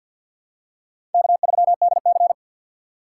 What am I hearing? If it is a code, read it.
Morse code, heard as a keyed tone, about 52 wpm, 700 Hz. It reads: K3DC
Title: USA Call Signs - 3x Character Spacing - 50wpm